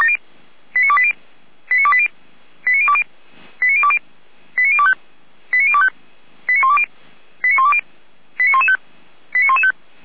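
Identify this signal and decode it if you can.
Five-tone selective call (ZVEI), heard as quick runs of five tones, tones of about 70 ms. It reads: 89280 89280 89020 89020 89025 89025 89120 89120 89115 89115
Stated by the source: ZVEI five-tone call 89262 received by SDR, channel 455, from feuerwehr-orlamuende.org fire brigade server.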